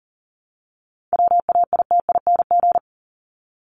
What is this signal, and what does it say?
Morse code, heard as a keyed tone, about 40 wpm, 700 Hz. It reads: WAITING
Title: ING Suffix - Top 100 Words with Base Words 40wpm